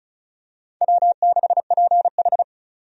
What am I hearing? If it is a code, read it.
Morse code, heard as a keyed tone, about 36 wpm, 700 Hz. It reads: W6PH
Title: USA Call Signs - 4 Characters - 35wpm